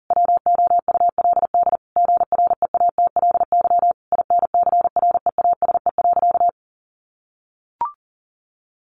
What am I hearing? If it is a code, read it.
Morse code, heard as a keyed tone, about 40 wpm, 700 Hz. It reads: WOULD GREATLY INCREASE.
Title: Sentences from Top 1400 Words 40wpm